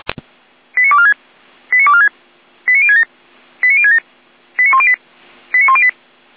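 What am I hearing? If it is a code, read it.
Five-tone selective call (ZVEI), heard as quick runs of five tones, tones of about 70 ms. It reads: 89246 89246 89067 89067 89108 89108
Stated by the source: ZVEI five-tone selective call received on an SDR.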